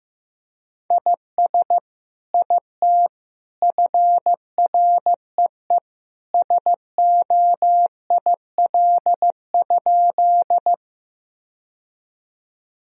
Morse code, heard as a keyed tone, about 15 wpm, 700 Hz. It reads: IS IT FREE SOIL?
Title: Sentences from Top 1000 Words - Review Part 4 - 15wpm